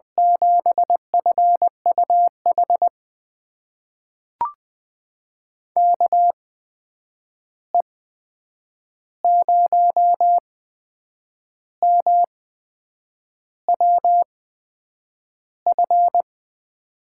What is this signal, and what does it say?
Morse code, heard as a keyed tone, about 20 wpm, spaced as for 8 wpm, 700 Hz. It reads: 7FUH KE0MWF
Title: USA Call Signs - 6 Characters - 8x Character Spacing - 20wpm